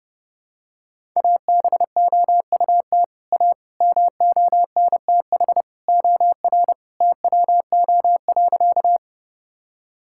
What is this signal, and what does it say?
Morse code, heard as a keyed tone, about 30 wpm, 700 Hz. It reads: ABOUT A MONTH OR TWO.
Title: Sentences from Top 1000 Words - Review Part 5 - 30wpm